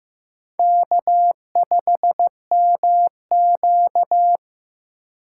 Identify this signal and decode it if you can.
Morse code, heard as a keyed tone, about 15 wpm, 700 Hz. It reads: K5MQ
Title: USA Call Signs - 2x Character Spacing - 15wpm